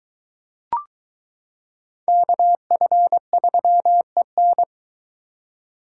Morse code, heard as a keyed tone, about 22 wpm, 700 Hz. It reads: KF3EN